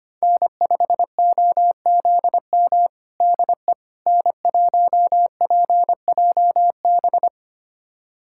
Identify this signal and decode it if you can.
Morse code, heard as a keyed tone, about 25 wpm, 700 Hz. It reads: N5OZM DE N1PJB
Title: USA Callsign DE Callsign - 5 Characters - 25wpm